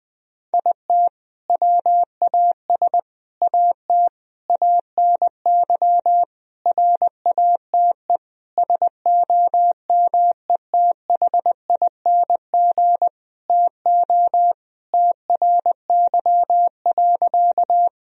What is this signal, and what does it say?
Morse code, heard as a keyed tone, about 21 wpm, 700 Hz. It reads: IT WAS AT ANY RATE SOMETHING TO TRY.
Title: Sentences from Top 1000 Words - Review Part 4 - 20wpm